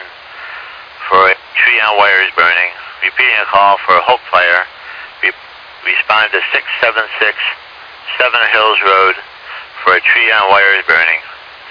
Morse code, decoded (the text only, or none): none